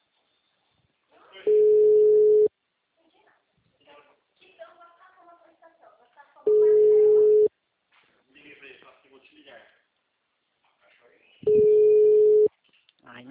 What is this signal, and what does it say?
Ringback tone (Europe, 425 Hz), with speech.